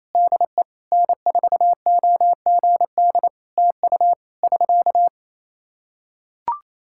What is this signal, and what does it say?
Morse code, heard as a keyed tone, about 27 wpm, 700 Hz. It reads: DE N4OGD TU <SK>
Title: Practice QSO - Exchange 4 - Ending - 28wpm